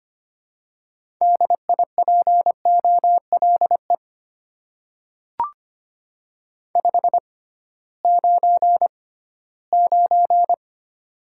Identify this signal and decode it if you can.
Morse code, heard as a keyed tone, about 25 wpm, spaced as for 16 wpm, 690 Hz. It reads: DIPOLE 599
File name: Top 90 QSO Elements - 6x Character Spacing - 25wpm